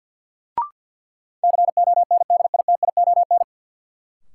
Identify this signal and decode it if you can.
Morse code, heard as a keyed tone, about 51 wpm, 700 Hz. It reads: CONDITION
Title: Sentences from Top 1000 Words - New Word Drill - 50wpm